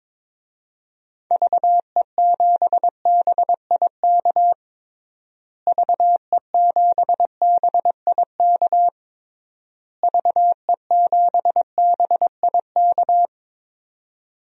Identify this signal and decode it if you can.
Morse code, heard as a keyed tone, about 22 wpm, 700 Hz. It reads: VE7BIK VE7BIK VE7BIK
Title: DXCC-100 Call Signs - 6 Characters - Repeated 3x - 22wpm